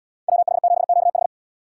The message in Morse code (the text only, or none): WH7QH